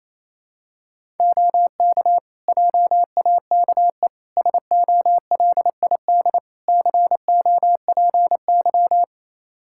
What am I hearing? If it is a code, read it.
Morse code, heard as a keyed tone, about 28 wpm, 700 Hz. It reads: OK JAKE SOLID COPY
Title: Practice QSO - Exchange 3 - More Info - 28wpm